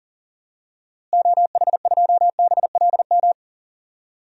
Morse code, heard as a keyed tone, about 40 wpm, 700 Hz. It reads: OH2BLM
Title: DXCC-100 Call Signs - 6 Characters - Repeated 2x - 40wpm